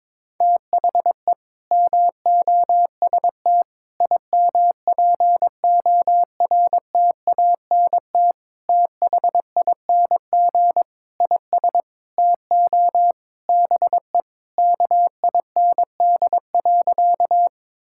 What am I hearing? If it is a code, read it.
Morse code, heard as a keyed tone, about 22 wpm, 700 Hz. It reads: THE MOST IMPORTANT THING IS TO BE KIND.
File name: practice / 22wpm / Sentences with 4-Word Phrases - 22wpm